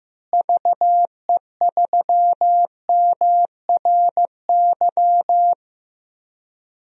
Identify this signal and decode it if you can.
Morse code, heard as a keyed tone, about 15 wpm, 700 Hz. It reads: VE3MRY
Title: DXCC-100 Call Signs - 6 Characters - Repeated 3x - 15wpm